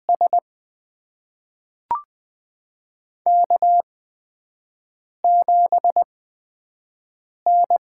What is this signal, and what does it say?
Morse code, heard as a keyed tone, about 20 wpm, spaced as for 5 wpm, 700 Hz. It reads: S K7N
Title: USA Call Signs - 8x Character Spacing - 20wpm